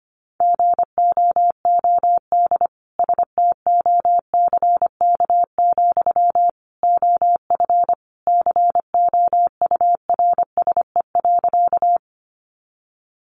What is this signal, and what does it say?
Morse code, heard as a keyed tone, about 25 wpm, 700 Hz. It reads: GOOD STOCK, OF COURSE.
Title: Sentences from Top 800 Words 25wpm